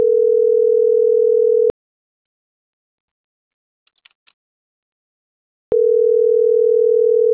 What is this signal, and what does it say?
Ringback tone (North America), 440 + 480 Hz.